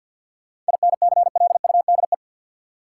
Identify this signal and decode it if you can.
Morse code, heard as a keyed tone, about 49 wpm, 700 Hz. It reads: INCLUDE
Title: Sentences from Top 1000 Words - Review Part 3 - New Word Drill - 50wpm